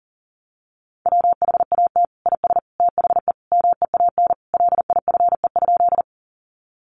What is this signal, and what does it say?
Morse code, heard as a keyed tone, about 37 wpm, 700 Hz. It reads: WHAT IS THE MEAN LIFE?
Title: Sentences from Top 900 Words 40wpm